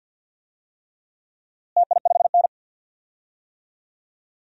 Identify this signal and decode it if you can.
Morse code, heard as a keyed tone, about 53 wpm, 700 Hz. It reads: TI5N
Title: DXCC-100 Call Signs - 4 Characters - Repeated 3x - 50wpm